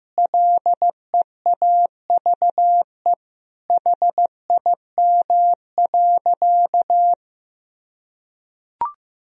Morse code, heard as a keyed tone, about 15 wpm, 700 Hz. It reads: LEAVE HIM.
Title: Sentences from Top 1000 Words - Review Part 4 - 15wpm